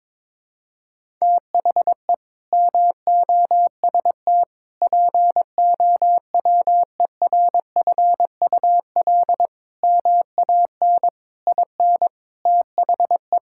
Morse code, heard as a keyed tone, about 21 wpm, 700 Hz. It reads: THE MOST POWERFUL MAN IN THE
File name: practice / 22wpm / Sets of 7 Words - 22wpm